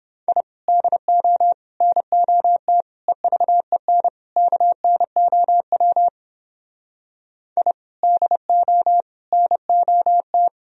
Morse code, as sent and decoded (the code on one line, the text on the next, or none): .. / -.. --- / -. --- - / . ...- . -. / -.- -. --- .-- / .. / -.. --- / -. --- -
I DO NOT EVEN KNOW I DO NOT